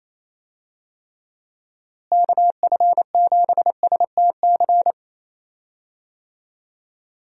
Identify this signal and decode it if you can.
Morse code, heard as a keyed tone, about 28 wpm, 700 Hz. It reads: KF7STC
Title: USA Call Signs - 6 Characters - Repeated 3x - 28wpm